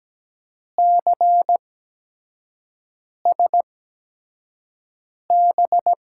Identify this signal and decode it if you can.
Morse code, heard as a keyed tone, about 17 wpm, spaced as for 5 wpm, 700 Hz. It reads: CSB
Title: USA Call Signs - 6 Characters - 8x Character Spacing - 17wpm